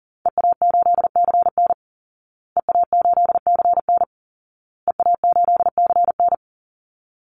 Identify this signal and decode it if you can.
Morse code, heard as a keyed tone, about 40 wpm, 700 Hz. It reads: EA8CN EA8CN EA8CN
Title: Call Signs Repeated 2x - 40wpm